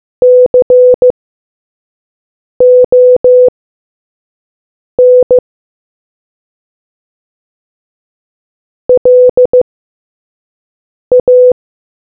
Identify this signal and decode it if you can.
Morse code, heard as a keyed tone, about 15 wpm, spaced as for 5 wpm, 500 Hz. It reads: CON LA